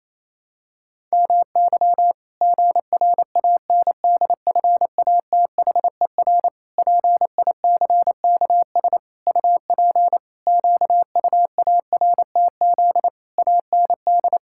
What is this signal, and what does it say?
Morse code, heard as a keyed tone, about 28 wpm, 700 Hz. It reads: MY GRANDFATHER PICKS UP QUARTZ AND
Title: Pangram Challenge v2 - 28wpm